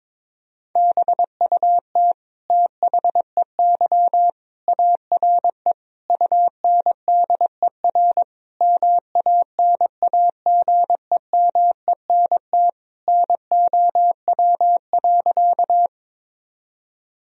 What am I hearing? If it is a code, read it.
Morse code, heard as a keyed tone, about 22 wpm, 700 Hz. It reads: BUT THEY ARE UNDER MANAGEMENT NOW.
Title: Sentences from Top 1000 Words - Review Part 4 - 22wpm